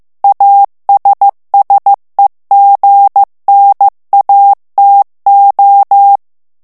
Morse code, heard as a keyed tone, about 15 wpm, 800 Hz. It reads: ASSEGNATO
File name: morse mp3 / ITALIEN texte clair 2 vitesse 14 mots minute